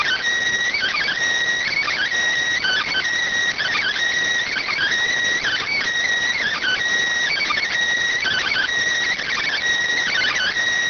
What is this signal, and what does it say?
Emergency Alert System (SAME) tones (header bursts), heard as harsh buzzing bursts. It is heard from 1.0 to 10.1 s.